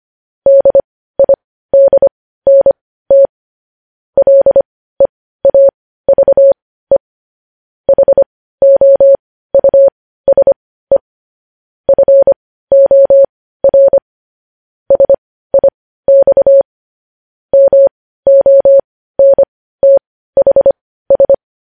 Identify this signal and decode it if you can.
Morse code, heard as a keyed tone, about 25 wpm, spaced as for 15 wpm, 550 Hz. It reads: DIDNT LEAVE HOUSE FOR SIX MONTHS
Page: News Headlines in Morse Code at 15 WPM